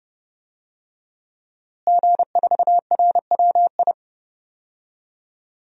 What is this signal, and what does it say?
Morse code, heard as a keyed tone, about 30 wpm, 700 Hz. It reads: G4RWI
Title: DXCC-100 Call Signs - 5 Characters - Repeated 2x - 30wpm